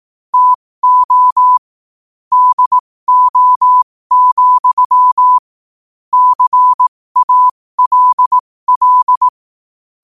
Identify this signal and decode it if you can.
Morse code, heard as a keyed tone, about 18 wpm, spaced as for 15 wpm, 1000 Hz. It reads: TO DO, CALL
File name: qotdfiles / 15wpm